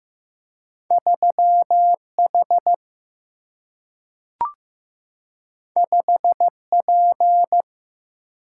Morse code, heard as a keyed tone, about 15 wpm, 700 Hz. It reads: 3 H 5 P
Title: Single-Letter Single-Number 15wpm